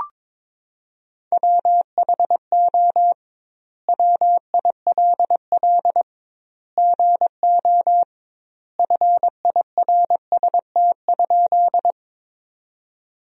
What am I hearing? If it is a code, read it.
Morse code, heard as a keyed tone, about 22 wpm, 700 Hz. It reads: WHO WILL GO FIRST?
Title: 2x Word-Spacing - Sentences from Top 100 Words - 22wpm